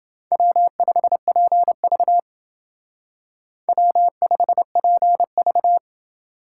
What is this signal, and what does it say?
Morse code, heard as a keyed tone, about 28 wpm, 700 Hz. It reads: W5PV W5PV